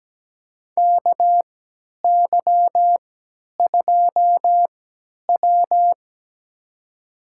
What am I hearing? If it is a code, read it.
Morse code, heard as a keyed tone, about 17 wpm, spaced as for 10 wpm, 700 Hz. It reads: KY2W
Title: USA Call Signs - 4 Characters - 3x Character Spacing - 17wpm